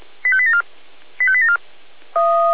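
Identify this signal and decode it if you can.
Five-tone selective call (ZVEI), heard as quick runs of five tones, tones of about 70 ms. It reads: 85874 85874
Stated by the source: ZVEI five-tone selective call received on an SDR, fire brigade.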